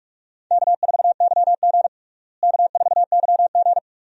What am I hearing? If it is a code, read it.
Morse code, heard as a keyed tone, about 41 wpm, 700 Hz. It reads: K4YG K4YG
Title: USA Call Signs - 4 Characters - Repeated 2x - 45wpm